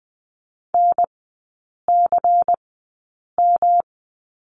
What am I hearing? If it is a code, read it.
Morse code, heard as a keyed tone, about 20 wpm, spaced as for 8 wpm, 700 Hz. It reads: NCM
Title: 107 Lesson 18e - Letter m - Three Characters 20wpm